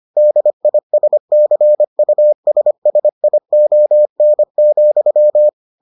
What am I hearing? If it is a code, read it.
Morse code, heard as a keyed tone, about 25 wpm, 600 Hz. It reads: DISCUSSION,